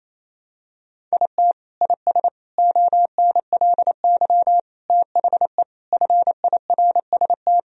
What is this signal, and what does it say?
Morse code, heard as a keyed tone, about 27 wpm, 690 Hz. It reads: IT IS ONLY THE FIRST
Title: Sentences from Top 1000 Words - Review Part 2 - 28wpm